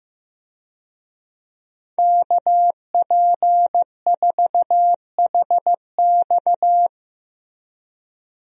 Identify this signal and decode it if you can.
Morse code, heard as a keyed tone, about 16 wpm, 700 Hz. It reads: KP4HX
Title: USA Call Signs - 5 Characters - Repeated 3x - 15wpm